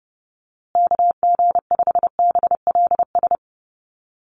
Morse code, heard as a keyed tone, about 30 wpm, 700 Hz. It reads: KG5BLS
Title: USA Call Signs Repeated 2x 30wpm